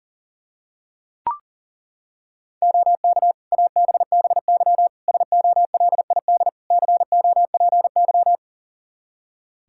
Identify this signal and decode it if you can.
Morse code, heard as a keyed tone, about 40 wpm, 700 Hz. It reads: OK ABBY SOLID COPY
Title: Practice QSO - Exchange 3 - More Info - 40wpm